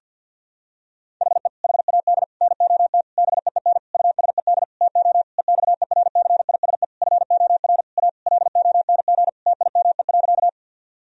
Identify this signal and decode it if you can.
Morse code, heard as a keyed tone, about 50 wpm, 700 Hz. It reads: HE HAD NOT BEEN USED TO EXERCISE FOR A LONG TIME.
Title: Sentences from Top 1000 Words - Review Part 4 - 50wpm